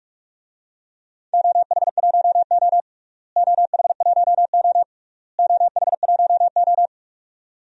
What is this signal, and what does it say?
Morse code, heard as a keyed tone, about 47 wpm, 700 Hz. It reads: OH1O OH1O OH1O
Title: DXCC-100 Call Signs - 4 Characters - Repeated 3x - 45wpm